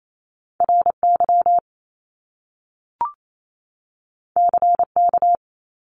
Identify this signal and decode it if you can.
Morse code, heard as a keyed tone, about 28 wpm, 700 Hz. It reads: RY CK
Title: Top Two Letter Combinations in Words 28wpm